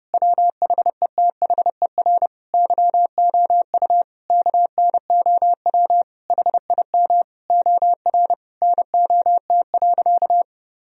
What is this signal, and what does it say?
Morse code, heard as a keyed tone, about 32 wpm, 700 Hz. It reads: WHETHER YOU KNOW HIM OR NOT.
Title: Sentences from Top 1000 Words - Review Part 3 - 30wpm